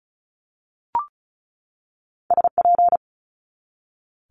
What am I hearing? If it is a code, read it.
Morse code, heard as a keyed tone, about 36 wpm, 700 Hz. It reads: SP
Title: Multi-letter Phonemes 35wpm